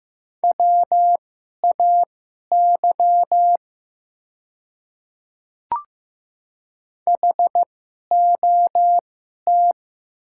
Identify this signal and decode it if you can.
Morse code, heard as a keyed tone, about 15 wpm, spaced as for 11 wpm, 700 Hz. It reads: WAY HOT
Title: 2x Character Spacing - Top 100 Words - 15wpm